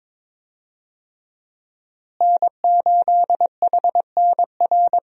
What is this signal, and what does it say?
Morse code, heard as a keyed tone, about 22 wpm, 700 Hz. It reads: N8HNR
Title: USA Call Signs - 5 Characters - Repeated 3x - 22wpm